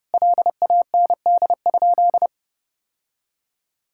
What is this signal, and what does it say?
Morse code, heard as a keyed tone, about 30 wpm, 700 Hz. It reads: LAND?
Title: Sentences from Top 1600 Words 30wpm